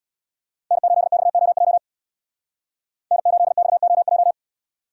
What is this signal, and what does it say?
Morse code, heard as a keyed tone, about 64 wpm, 700 Hz. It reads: N8XOJ N8XOJ